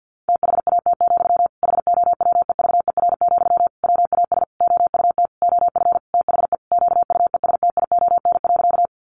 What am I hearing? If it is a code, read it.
Morse code, heard as a keyed tone, about 50 wpm, 700 Hz. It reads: THAT, HOWEVER, WAS OUT OF THE QUESTION.
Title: Sentences from Top 600 Words 50wpm